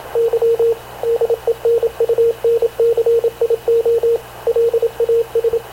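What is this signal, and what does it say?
Morse code, heard as a keyed tone, about 26 wpm, 460 Hz, with radio noise. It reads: Y DENUNCIO LAS
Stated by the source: Havana Radio / CLA sending Spanish press (PX) in Morse, recorded by Radio Officer Paul Dumesnil onboard CNAV QUEST/CZDO.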